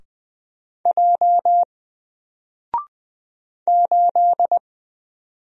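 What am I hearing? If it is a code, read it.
Morse code, heard as a keyed tone, about 20 wpm, 700 Hz. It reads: J 8
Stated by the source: Single letter and number Morse drill at 20 wpm, from N4LCD.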